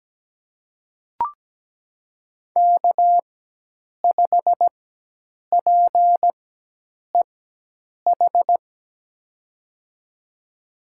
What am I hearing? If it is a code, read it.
Morse code, heard as a keyed tone, about 17 wpm, spaced as for 8 wpm, 700 Hz. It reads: K5PEH
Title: USA Call Signs - 4x Character Spacing - 17wpm